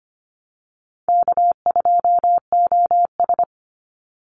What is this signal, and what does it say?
Morse code, heard as a keyed tone, about 25 wpm, 700 Hz. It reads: K2OS
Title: Easy Call Signs Repeated 3x - 25wpm